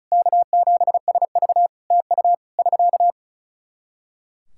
Morse code, heard as a keyed tone, about 36 wpm, 700 Hz. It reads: K7SV TU <SK>
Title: Practice QSO - Exchange 4 - Ending - 35wpm